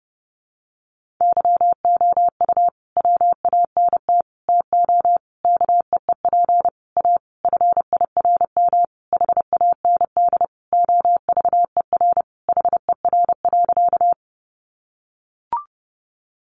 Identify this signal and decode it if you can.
Morse code, heard as a keyed tone, about 30 wpm, 700 Hz. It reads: YOU WANT TO KEEP A FIRM HAND OVER HER.
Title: Sentences from Top 1400 Words 30wpm